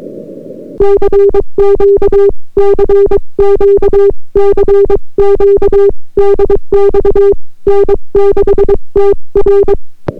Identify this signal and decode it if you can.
Morse code, heard as a keyed tone, about 23 wpm, 400 Hz. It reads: CQ CQ CQ DX N6TR